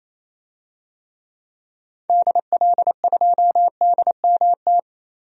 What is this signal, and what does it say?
Morse code, heard as a keyed tone, about 28 wpm, 700 Hz. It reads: DL2DMT
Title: DXCC-100 Call Signs - 6 Characters - Repeated 2x - 28wpm